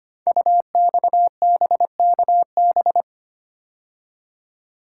Morse code, heard as a keyed tone, about 25 wpm, 690 Hz. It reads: UXBKB
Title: Code Groups - 5 Letters - 25wpm